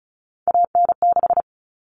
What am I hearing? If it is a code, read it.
Morse code, heard as a keyed tone, about 34 wpm, 700 Hz. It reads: AN6
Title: All Cut Numbers - 3-Digit 35wpm